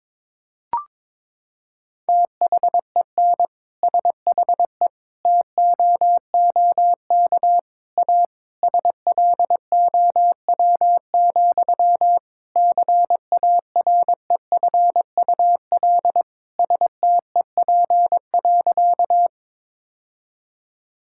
Morse code, heard as a keyed tone, about 22 wpm, 700 Hz. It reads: THEN SHE TOOK A SLOW, CAREFUL STEP.
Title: Sentences from Top 1600 Words 22wpm